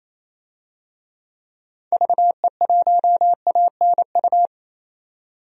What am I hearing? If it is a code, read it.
Morse code, heard as a keyed tone, about 28 wpm, 700 Hz. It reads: VE1ANU
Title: DXCC-100 Call Signs - 6 Characters - 28wpm